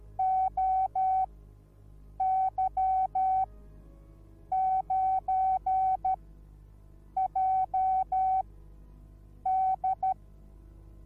Morse code, heard as a keyed tone, about 13 wpm, spaced as for 6 wpm, 760 Hz. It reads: OY9JD